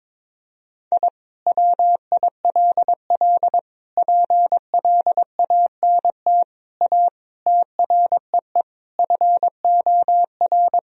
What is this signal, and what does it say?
Morse code, heard as a keyed tone, about 22 wpm, 700 Hz. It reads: I WILL PLANT A TREE FOR
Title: Sentences from Top 1000 Words - Review Part 1 - 22wpm